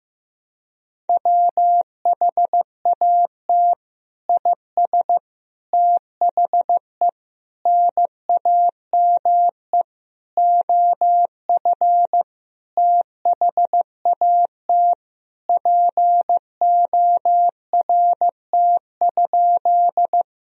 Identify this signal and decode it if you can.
Morse code, heard as a keyed tone, about 15 wpm, 700 Hz. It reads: WHAT IS THE NAME OF THAT PORT?
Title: Sentences from Top 1000 Words - Review Part 1 - 15wpm